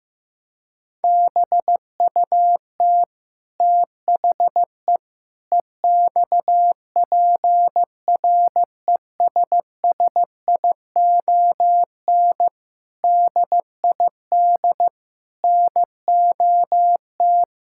Morse code, heard as a keyed tone, about 15 wpm, 700 Hz. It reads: BUT THE EXPRESSION DID NOT
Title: Sentences from Top 1600 Words 15wpm